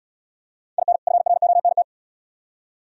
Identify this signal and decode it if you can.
Morse code, heard as a keyed tone, about 76 wpm, 700 Hz. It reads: IN FRONT